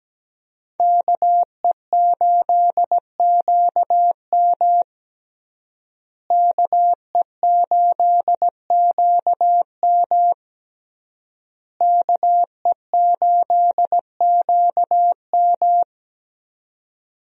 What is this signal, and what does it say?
Morse code, heard as a keyed tone, about 17 wpm, 700 Hz. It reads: KE8QM KE8QM KE8QM
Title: USA Call Signs - 5 Characters - Repeated 3x - 17wpm